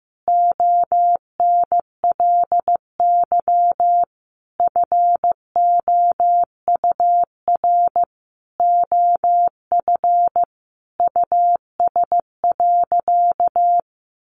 Morse code, heard as a keyed tone, about 15 wpm, 700 Hz. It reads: ONLY FOUR OF US.
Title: Sentences from Top 300 Words 15wpm